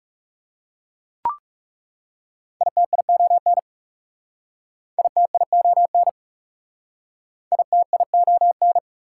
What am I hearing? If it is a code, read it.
Morse code, heard as a keyed tone, about 37 wpm, 690 Hz. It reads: ITION ITION ITION
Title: Speed-Racing Prefixes and Suffixes 30wpm